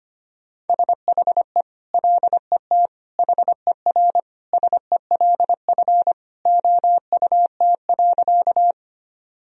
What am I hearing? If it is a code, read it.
Morse code, heard as a keyed tone, about 25 wpm, 700 Hz. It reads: SHE LET HERSELF OUT.
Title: Sentences from Top 1000 Words - Review Part 1 - 25wpm